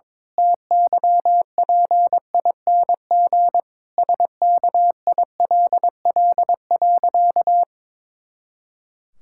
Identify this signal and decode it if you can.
Morse code, heard as a keyed tone, about 22 wpm, 700 Hz. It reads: TYPING SKILL.
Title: Pangram Challenge v2 - 22wpm